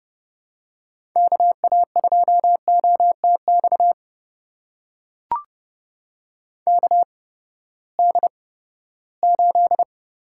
Morse code, heard as a keyed tone, about 32 wpm, spaced as for 17 wpm, 700 Hz. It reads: KA2OTX KD8
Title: USA Call Signs - 6 Characters - 8x Character Spacing - 30wpm